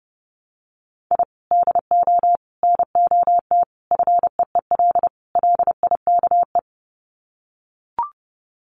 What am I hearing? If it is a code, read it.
Morse code, heard as a keyed tone, about 30 wpm, 700 Hz. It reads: I DO NOT FEEL LIKE
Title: Sets of 5 Words Encore 30wpm